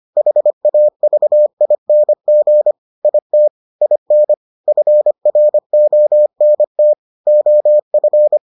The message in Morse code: HAVING IT IN FRONT OF

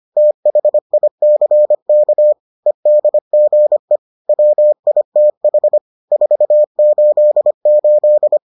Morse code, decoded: THICK EDGE WITH 488

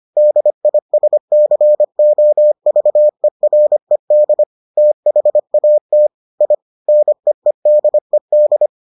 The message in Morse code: DISCOVERED THAT I NEEDED

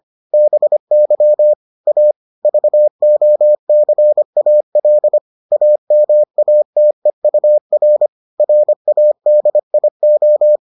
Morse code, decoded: BY A VOCAL AMATEUR RADIO

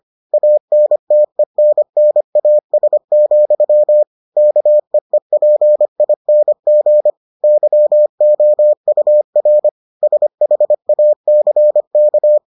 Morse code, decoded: ANTENNAS, KEEPING YOUR SHACK

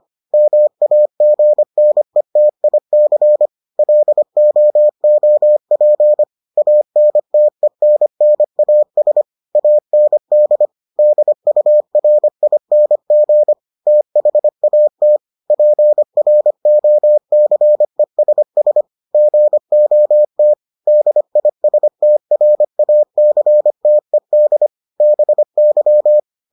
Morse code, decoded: MAGNETIC LOOP ANTENNAS AND DURING THAT PROCESS GOT DISTRACTED BY

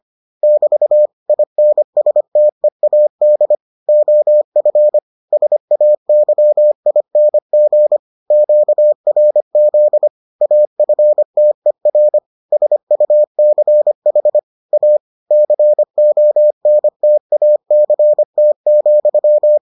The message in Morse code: <BT> INSTEAD OF SAYING QRZ AFTER SUCH A CONTACT,

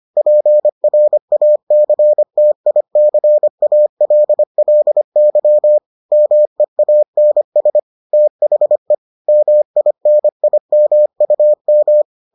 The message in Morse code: PRACTICALLY MEANS THE MINIMUM

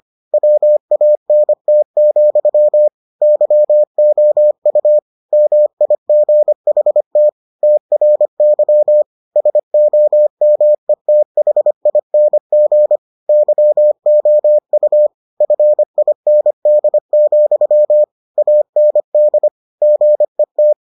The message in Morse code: WANT, YOU MIGHT TRY SOMETHING YOU FIND, AND GET

